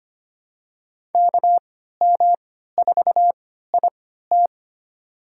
Morse code, heard as a keyed tone, about 25 wpm, spaced as for 14 wpm, 700 Hz. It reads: KM4IT
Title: USA Call Signs - 5 Characters - 3x Character Spacing - 25wpm